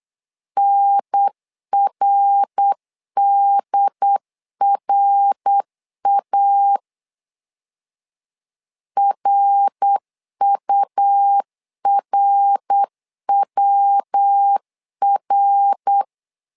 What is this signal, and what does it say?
Morse code, heard as a keyed tone, about 8 wpm, 800 Hz. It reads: NRDRA RURWR
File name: LEZIONI / 3 MWRO / 3 MWR 8wpm 2444